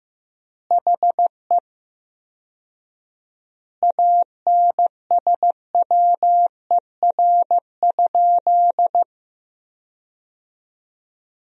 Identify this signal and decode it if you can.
Morse code, heard as a keyed tone, about 16 wpm, 700 Hz. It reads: HE ANSWER?
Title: 4x Word-Spacing - Sentences from Top 300 Words - 15wpm